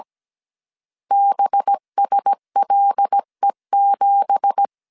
Morse code, heard as a keyed tone, about 17 wpm, 800 Hz. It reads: BSLE7
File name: CW / DET / 1 9 1500C S C 16wpm 2909